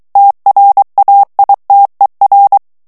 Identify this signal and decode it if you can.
Morse code, heard as a keyed tone, about 23 wpm, 800 Hz. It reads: TRAITER
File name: morse mp3 / FRANCAIS TEXTE CLAIR 2 VITESSE 22 MOTS MINUTES